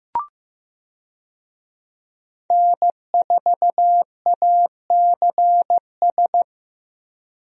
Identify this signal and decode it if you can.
Morse code, heard as a keyed tone, about 15 wpm, 700 Hz. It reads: N4ACS
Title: USA Call Signs - 5 Characters - 15wpm